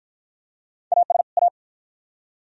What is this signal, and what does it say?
Morse code, heard as a keyed tone, about 50 wpm, 700 Hz. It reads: AS A